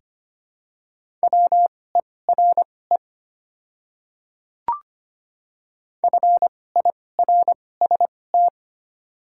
Morse code, heard as a keyed tone, about 25 wpm, spaced as for 18 wpm, 690 Hz. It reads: WERE FIRST
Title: 2x Character Spacing - Top 200 Words - 25wpm